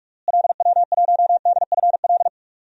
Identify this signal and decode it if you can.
Morse code, heard as a keyed tone, about 46 wpm, 700 Hz. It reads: LW1DFL
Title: DXCC-100 Call Signs - 6 Characters - Repeated 3x - 45wpm